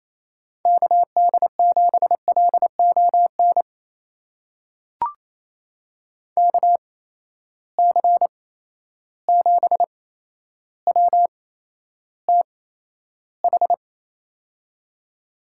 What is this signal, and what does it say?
Morse code, heard as a keyed tone, about 27 wpm, spaced as for 12 wpm, 700 Hz. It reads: KD7LON KC7WTH
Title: USA Call Signs - 6 Characters - 8x Character Spacing - 28wpm